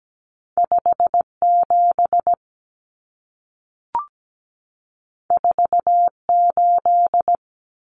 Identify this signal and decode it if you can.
Morse code, heard as a keyed tone, about 17 wpm, 700 Hz. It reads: 57 48
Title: Common Cut Numbers - 2-Digit 17wpm